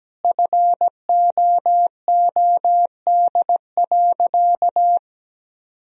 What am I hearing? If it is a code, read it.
Morse code, heard as a keyed tone, about 17 wpm, 700 Hz. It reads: FOOD.
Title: Sentences from Top 1000 Words - Review Part 4 - 17wpm